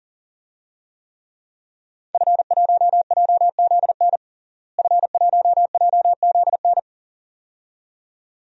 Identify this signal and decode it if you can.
Morse code, heard as a keyed tone, about 40 wpm, 700 Hz. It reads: F1JZN F1JZN
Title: DXCC-100 Call Signs - 5 Characters - Repeated 2x - 40wpm